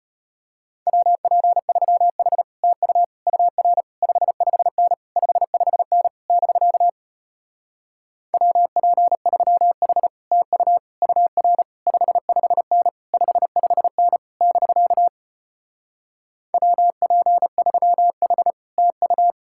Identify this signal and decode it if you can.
Morse code, heard as a keyed tone, about 33 wpm, 700 Hz. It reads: WP3H TU UR 55N 55N <BK> WP3H TU UR 55N 55N <BK> WP3H TU